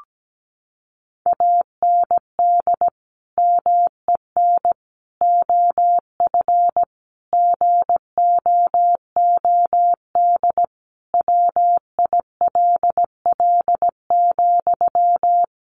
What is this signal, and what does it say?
Morse code, heard as a keyed tone, about 17 wpm, 700 Hz. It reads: AND MEN OF GOOD WILL,
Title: Sentences from Top 1400 Words 17wpm